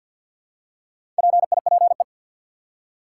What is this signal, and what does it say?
Morse code, heard as a keyed tone, about 53 wpm, 700 Hz. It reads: PIPE
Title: Four Letter Words - 50wpm